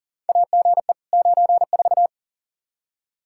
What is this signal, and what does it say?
Morse code, heard as a keyed tone, about 40 wpm, 700 Hz. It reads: AGE 94
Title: K1USN SST QSO Practice 40wpm